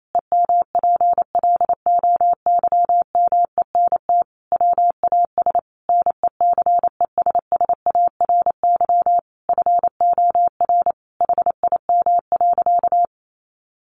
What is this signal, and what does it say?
Morse code, heard as a keyed tone, about 28 wpm, 700 Hz. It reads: EMPLOYMENT WAS NECESSARY FOR HIM.
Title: Sentences from Top 1500 Words 28wpm